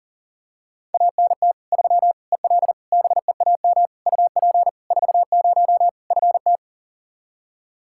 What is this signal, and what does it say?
Morse code, heard as a keyed tone, about 40 wpm, 700 Hz. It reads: ANT 3 EL BEAM UP 40 FT
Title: Practice QSO - Exchange 2 - Info - 40wpm